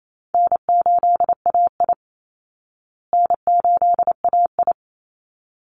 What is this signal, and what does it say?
Morse code, heard as a keyed tone, about 28 wpm, 700 Hz. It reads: N8AI N8AI
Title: Call Signs Repeated 3x - 28wpm